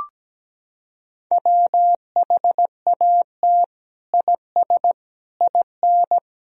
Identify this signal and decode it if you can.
Morse code, heard as a keyed tone, about 17 wpm, 700 Hz. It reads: WHAT IS IN
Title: Sentences from Top 200 Words and Abbreviations 17wpm